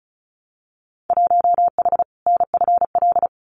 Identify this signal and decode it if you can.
Morse code, heard as a keyed tone, about 34 wpm, 700 Hz. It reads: 1H NFL
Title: Winter Field Day Exchanges 35wpm